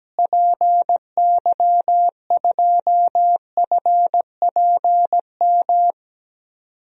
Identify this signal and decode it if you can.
Morse code, heard as a keyed tone, about 17 wpm, 700 Hz. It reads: PY2FPM
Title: DXCC-100 Call Signs - 6 Characters - Repeated 2x - 17wpm